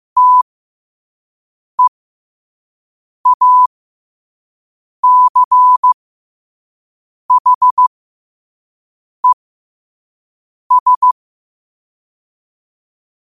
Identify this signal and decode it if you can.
Morse code, heard as a keyed tone, about 15 wpm, spaced as for 5 wpm, 1000 Hz. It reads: TEACHES
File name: qotdfiles / 5wpm